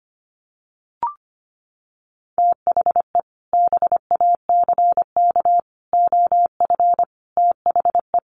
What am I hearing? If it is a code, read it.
Morse code, heard as a keyed tone, about 25 wpm, 700 Hz. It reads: THE BACK OF THE
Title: Sets of 5 Words Encore2 25wpm